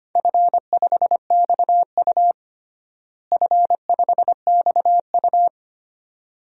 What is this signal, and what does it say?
Morse code, heard as a keyed tone, about 25 wpm, 700 Hz. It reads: F5XU F5XU
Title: DXCC-100 Call Signs - 4 Characters - Repeated 2x - 25wpm